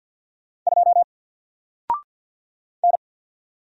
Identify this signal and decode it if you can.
Morse code, heard as a keyed tone, about 50 wpm, 700 Hz. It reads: 2 N
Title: Single Letter-Number 50wpm